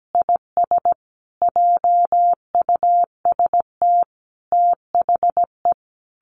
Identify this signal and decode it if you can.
Morse code, heard as a keyed tone, about 17 wpm, 700 Hz. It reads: IS JUST THE
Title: Sentences from Top 1500 Words 17wpm